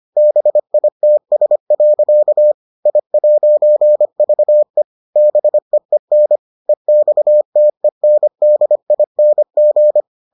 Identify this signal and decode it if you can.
Morse code, heard as a keyed tone, about 25 wpm, 600 Hz. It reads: BITS. I'VE BEEN EXTENDING